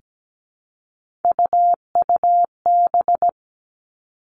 Morse code, heard as a keyed tone, about 17 wpm, 700 Hz. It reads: UUB
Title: Three Challenging Letter-Number Combinations 17wpm